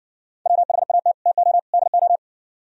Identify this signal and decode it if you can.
Morse code, heard as a keyed tone, about 64 wpm, 700 Hz. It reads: WHAT TO DO